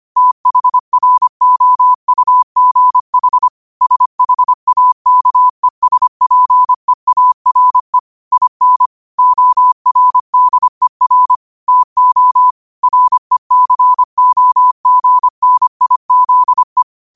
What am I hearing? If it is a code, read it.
Morse code, heard as a keyed tone, about 25 wpm, 1000 Hz. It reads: THROUGH SHAKESPEARE IN ORDER TO RECOGNIZE